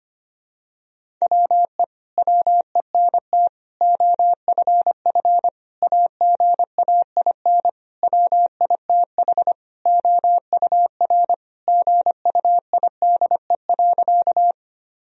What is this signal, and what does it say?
Morse code, heard as a keyed tone, about 25 wpm, 700 Hz. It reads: WE WENT OFF AGAIN WITH OUR GUIDE.
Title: Sentences from Top 1000 Words - Review Part 3 - 25wpm